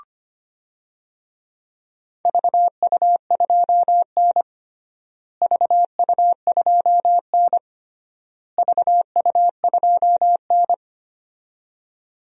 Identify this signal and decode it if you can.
Morse code, heard as a keyed tone, about 25 wpm, 700 Hz. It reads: VU2N VU2N VU2N
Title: DXCC-100 Call Signs - 4 Characters - Repeated 3x - 25wpm